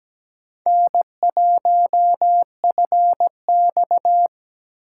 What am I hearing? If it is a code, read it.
Morse code, heard as a keyed tone, about 17 wpm, 700 Hz. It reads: N1FX
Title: USA Call Signs - 4 Characters - 2x Character Spacing - 17wpm